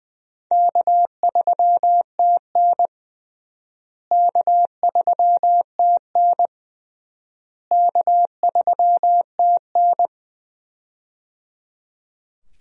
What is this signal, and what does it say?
Morse code, heard as a keyed tone, about 20 wpm, 700 Hz. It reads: K3TN K3TN K3TN
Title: USA Call Signs - 4 Characters - Repeated 3x - 20wpm